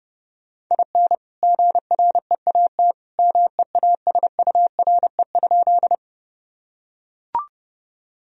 Morse code, heard as a keyed tone, about 32 wpm, 700 Hz. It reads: IN GREAT MEASURE?
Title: Sentences from Top 1000 Words - Review Part 2 - 30wpm